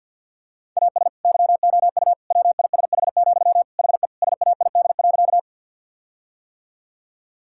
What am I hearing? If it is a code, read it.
Morse code, heard as a keyed tone, about 50 wpm, 700 Hz. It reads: AS YOU WISH, HE SAID.
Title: Sentences from Top 1600 Words 50wpm